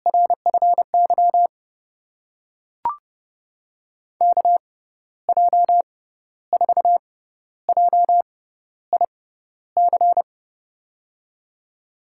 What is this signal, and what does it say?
Morse code, heard as a keyed tone, about 32 wpm, spaced as for 13 wpm, 700 Hz. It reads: RFY KJ4JIC
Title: USA Call Signs - 6 Characters - 6x Character Spacing - 30wpm